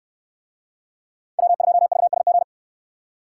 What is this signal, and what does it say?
Morse code, heard as a keyed tone, about 62 wpm, 700 Hz. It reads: W2VSG